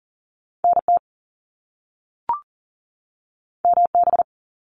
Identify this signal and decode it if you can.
Morse code, heard as a keyed tone, about 40 wpm, 700 Hz. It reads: NT MB
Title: Canadian Provinces and Territories 40wpm